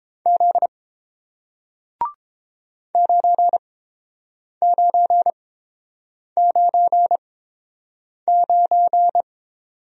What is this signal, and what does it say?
Morse code, heard as a keyed tone, about 30 wpm, spaced as for 7 wpm, 700 Hz. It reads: Z 9999